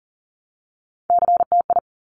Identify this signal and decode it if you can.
Morse code, heard as a keyed tone, about 42 wpm, 700 Hz. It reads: CTI
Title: Top Three Letter Combinations in Words 40wpm